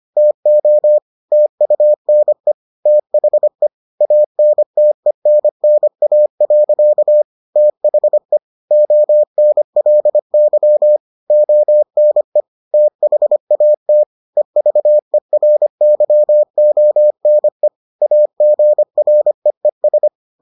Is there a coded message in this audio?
Yes: Morse code from 0.2 to 20.1 s, heard as a keyed tone, about 25 wpm, 600 Hz. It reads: TO TUNE THE ANTENNA. THE ONLY ONE THAT EVERYONE AGREES